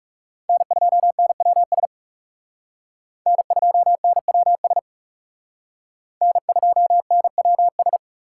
Morse code, heard as a keyed tone, about 36 wpm, 700 Hz. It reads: N2NWS N2NWS N2NWS
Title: Speed Racing - USA 1x3 Call Signs 30wpm